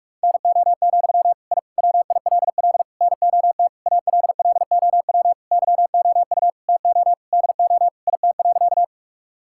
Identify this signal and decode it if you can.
Morse code, heard as a keyed tone, about 41 wpm, 700 Hz. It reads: NO, I WILL NOT ALLOW YOU TO DO IT.
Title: Sentences from Top 1000 Words - Review Part 5 - 45wpm